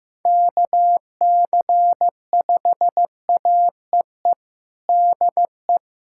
Morse code, heard as a keyed tone, about 15 wpm, 700 Hz. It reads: KC5AEE DE
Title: USA Callsign DE Callsign - 6 Characters - 15wpm